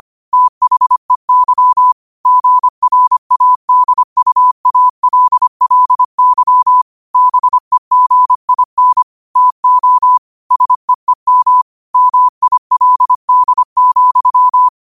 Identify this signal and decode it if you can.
Morse code, heard as a keyed tone, about 25 wpm, 1000 Hz. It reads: THEY GRADUALLY BEGIN TO SEEM MILD,
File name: qotdfiles / 25wpm